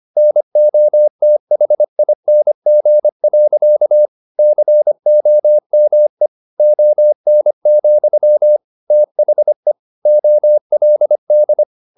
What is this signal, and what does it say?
Morse code, heard as a keyed tone, about 25 wpm, 600 Hz. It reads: NOTHING. COME ON, THE OLD
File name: the old man and the sea part 1 25wpm